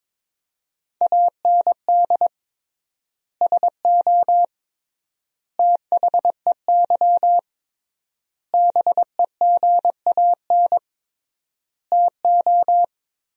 Morse code, heard as a keyed tone, about 22 wpm, 700 Hz. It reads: AND SO THEY BEGAN TO
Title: 3x Word-Spacing - Sentences from Top 400 Words - 22wpm